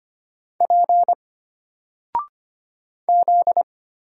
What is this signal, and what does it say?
Morse code, heard as a keyed tone, about 25 wpm, 700 Hz. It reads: P Z
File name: Single-Letter-Number-25wpm